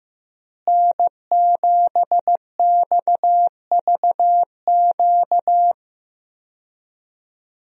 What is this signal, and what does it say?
Morse code, heard as a keyed tone, about 15 wpm, 700 Hz. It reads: N7XVQ
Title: USA Call Signs - 5 Characters - Repeated 2x - 15wpm